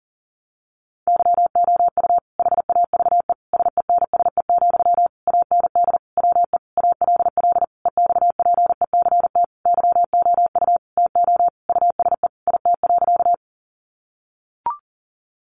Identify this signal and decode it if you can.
Morse code, heard as a keyed tone, about 40 wpm, 700 Hz. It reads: YOU HAVE SENSE, AND WE ALL EXPECT YOU TO USE IT.
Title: Sentences from Top 1100 Words 40wpm